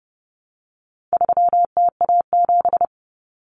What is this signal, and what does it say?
Morse code, heard as a keyed tone, about 30 wpm, 700 Hz. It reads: 3TA7
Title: Common Cut Numbers - 4-Digit 30wpm